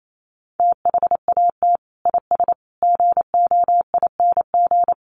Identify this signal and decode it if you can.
Morse code, heard as a keyed tone, about 28 wpm, 700 Hz. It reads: THAT IS GOING
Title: Sets of 5 Words Encore2 28wpm